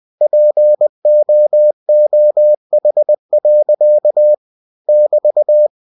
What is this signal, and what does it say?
Morse code, heard as a keyed tone, about 20 wpm, 600 Hz. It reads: POOH. <BT>
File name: cw-books / winnie-the-pooh / winnie-20wpm-page-0108